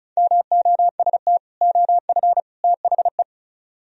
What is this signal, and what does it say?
Morse code, heard as a keyed tone, about 43 wpm, 700 Hz. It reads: MOST OF THE
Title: Speed-Racing - Sets of 3 Words 35wpm